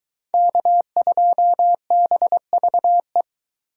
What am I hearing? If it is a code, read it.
Morse code, heard as a keyed tone, about 23 wpm, 700 Hz. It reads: K2BVE